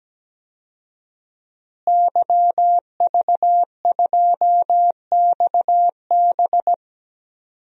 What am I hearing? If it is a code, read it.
Morse code, heard as a keyed tone, about 17 wpm, 700 Hz. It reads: YV2XB
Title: DXCC-100 Call Signs - 5 Characters - Repeated 2x - 17wpm